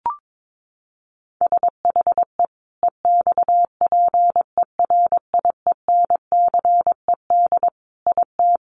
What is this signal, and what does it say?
Morse code, heard as a keyed tone, about 22 wpm, 700 Hz. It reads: SHE EXPERIENCED IT
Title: Sentences from Top 1100 Words 22wpm